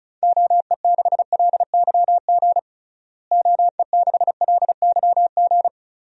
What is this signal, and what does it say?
Morse code, heard as a keyed tone, about 34 wpm, 700 Hz. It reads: OE6LYG OE6LYG
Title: DXCC-100 Call Signs - 6 Characters - Repeated 2x - 35wpm